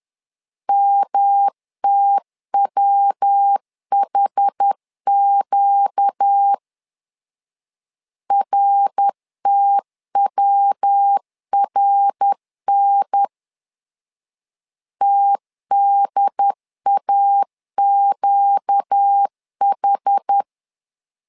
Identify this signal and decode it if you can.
Morse code, heard as a keyed tone, about 10 wpm, 800 Hz. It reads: MTWHQ RTWRN TDAQH
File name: LEZIONI / 4 Q 10wpm 2210